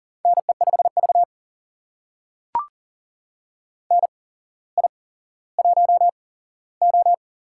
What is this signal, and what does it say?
Morse code, heard as a keyed tone, about 40 wpm, spaced as for 17 wpm, 700 Hz. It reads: NE5V NI1O
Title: USA Call Signs - 4 Characters - 8x Character Spacing - 40wpm